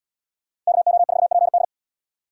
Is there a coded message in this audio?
Yes: Morse code from 0.7 to 1.6 s, heard as a keyed tone, about 68 wpm, 700 Hz. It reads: KO4PD